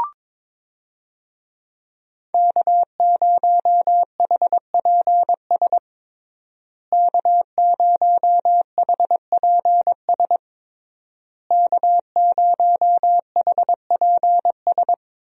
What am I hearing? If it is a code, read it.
Morse code, heard as a keyed tone, about 22 wpm, 700 Hz. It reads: K0HPS K0HPS K0HPS
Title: USA Call Signs - 5 Characters - Repeated 3x - 22wpm